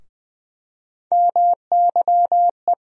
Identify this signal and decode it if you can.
Morse code, heard as a keyed tone, about 20 wpm, 700 Hz. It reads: MYE
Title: USA Call Sign Postfixes - 3 Letters 20wpm